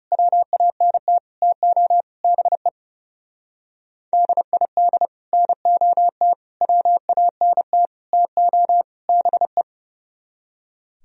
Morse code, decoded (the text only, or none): WANT TO BE DID NOT WANT TO BE